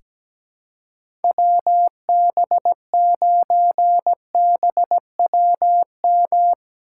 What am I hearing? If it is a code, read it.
Morse code, heard as a keyed tone, about 17 wpm, 700 Hz. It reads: WB9BWM
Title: USA Call Signs - 6 Characters - 3x Character Spacing - 17wpm